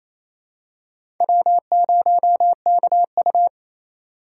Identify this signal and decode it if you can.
Morse code, heard as a keyed tone, about 28 wpm, 700 Hz. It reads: W0KU
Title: USA Call Signs - 4 Characters - 4x Character Spacing - 28wpm